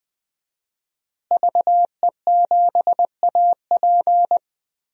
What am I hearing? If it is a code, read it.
Morse code, heard as a keyed tone, about 20 wpm, 700 Hz. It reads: VE7AP
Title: DXCC-100 Call Signs - 5 Characters - Repeated 3x - 20wpm